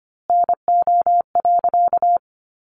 Morse code, heard as a keyed tone, about 25 wpm, 700 Hz. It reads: NO.
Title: A Sound Of Thunder 25wpm